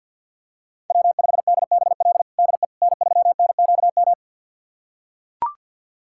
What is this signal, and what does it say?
Morse code, heard as a keyed tone, about 45 wpm, 700 Hz. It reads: W5DBL DE N3NQM
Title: USA Callsign DE Callsign - 5 Characters - 50wpm